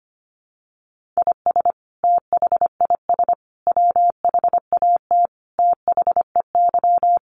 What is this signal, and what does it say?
Morse code, heard as a keyed tone, about 25 wpm, 700 Hz. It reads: IS THIS WHAT THEY
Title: Sentences from Top 1500 Words 25wpm